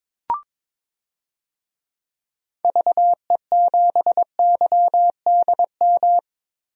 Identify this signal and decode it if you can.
Morse code, heard as a keyed tone, about 22 wpm, 700 Hz. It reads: VE7YDM
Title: DXCC-100 Call Signs - 6 Characters - Repeated 2x - 22wpm